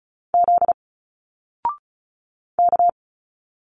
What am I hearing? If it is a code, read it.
Morse code, heard as a keyed tone, about 33 wpm, 700 Hz. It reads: Z K
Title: Single Letters 35wpm